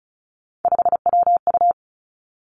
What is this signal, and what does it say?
Morse code, heard as a keyed tone, about 36 wpm, 700 Hz. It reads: 5WU
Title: Three Challenging Letter-Number Combinations 35wpm